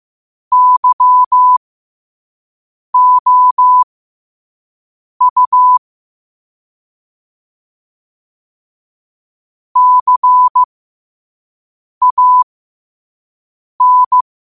Morse code, heard as a keyed tone, about 15 wpm, spaced as for 5 wpm, 1000 Hz. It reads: YOU CAN
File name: qotdfiles / 5wpm